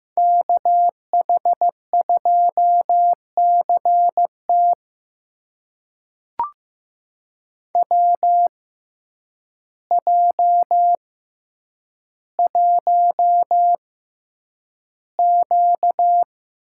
Morse code, heard as a keyed tone, about 15 wpm, spaced as for 8 wpm, 700 Hz. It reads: KH2CT WJ1Q
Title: USA Call Signs - 6x Character Spacing - 15wpm